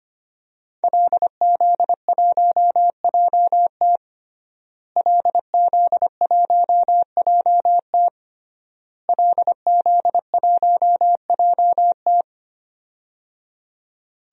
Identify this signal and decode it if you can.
Morse code, heard as a keyed tone, about 25 wpm, 700 Hz. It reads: LZ1JT LZ1JT LZ1JT
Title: DXCC-100 Call Signs - 5 Characters - Repeated 3x - 25wpm